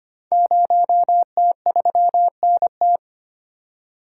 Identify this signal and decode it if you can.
Morse code, heard as a keyed tone, about 25 wpm, 700 Hz. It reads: 0T3NT
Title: Code Groups - 5 Letter:Numbers - 25wpm